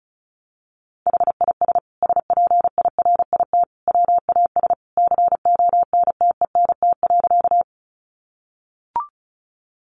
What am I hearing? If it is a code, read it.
Morse code, heard as a keyed tone, about 34 wpm, 700 Hz. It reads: HIS SPIRIT WAS CONTENT.
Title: Sentences from Top 1400 Words 35wpm